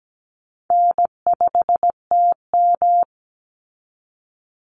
Morse code, heard as a keyed tone, about 17 wpm, 700 Hz. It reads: N5TM
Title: Easy Call Signs Repeated 3x - 17wpm